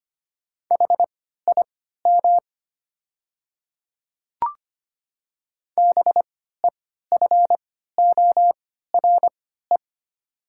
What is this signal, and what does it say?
Morse code, heard as a keyed tone, about 25 wpm, spaced as for 14 wpm, 700 Hz. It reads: HIM BEFORE
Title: 3x Character Spacing - Top 200 Words - 25wpm